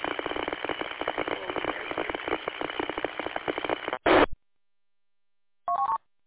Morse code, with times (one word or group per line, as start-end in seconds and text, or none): none